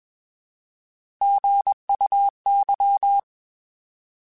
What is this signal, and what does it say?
Morse code, heard as a keyed tone, about 21 wpm, 800 Hz. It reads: GUY
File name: Alphabetical / Alphabetical mp3s / Gs @20